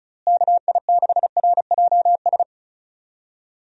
Morse code, heard as a keyed tone, about 36 wpm, 700 Hz. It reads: KI6RJS
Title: USA Call Signs - 6 Characters - Repeated 2x - 35wpm